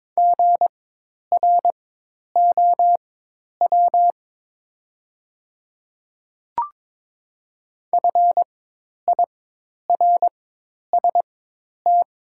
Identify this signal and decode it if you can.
Morse code, heard as a keyed tone, about 22 wpm, spaced as for 10 wpm, 700 Hz. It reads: GROW FIRST
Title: 4x Character Spacing - Top 300 Words - 22wpm